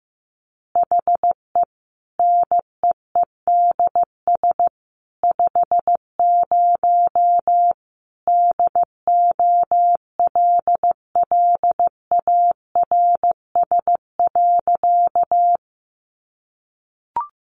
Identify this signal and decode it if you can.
Morse code, heard as a keyed tone, about 15 wpm, 700 Hz. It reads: HE NEEDS 50 DOLLARS.
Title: Additional Easy Phrases 15wpm